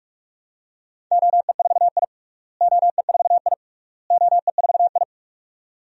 Morse code, heard as a keyed tone, about 46 wpm, 700 Hz. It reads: OE4I OE4I OE4I
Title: DXCC-100 Call Signs - 4 Characters - Repeated 3x - 45wpm